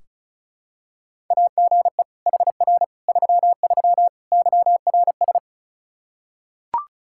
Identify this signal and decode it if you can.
Morse code, heard as a keyed tone, about 36 wpm, 700 Hz. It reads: AGE HR 33 YRS
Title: K1USN SST QSO Practice 35wpm